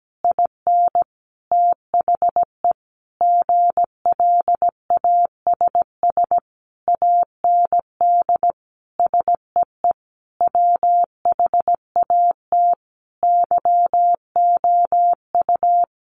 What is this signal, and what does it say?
Morse code, heard as a keyed tone, about 17 wpm, 700 Hz. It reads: IN THE GLASS AND SEE WHAT YOU
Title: Sentences from Top 1300 Words 17wpm